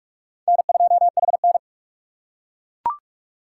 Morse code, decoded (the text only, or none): N2HN